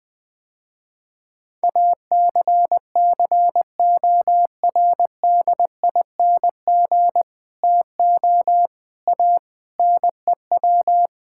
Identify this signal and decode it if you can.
Morse code, heard as a keyed tone, about 20 wpm, 700 Hz. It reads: ACCORDING TO A NEW
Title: Sentences with 7-Word Phrases - Spoken Beginning - 20wpm